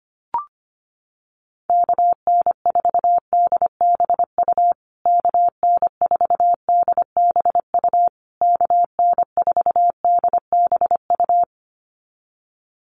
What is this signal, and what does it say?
Morse code, heard as a keyed tone, about 25 wpm, 700 Hz. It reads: KN4DBU KN4DBU KN4DBU
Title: USA Call Signs Repeated 3x 25wpm